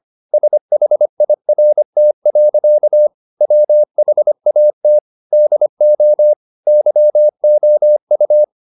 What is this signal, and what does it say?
Morse code, heard as a keyed tone, about 25 wpm, 600 Hz. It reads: SHIRT. WHAT DO YOU
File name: the old man and the sea part 1 25wpm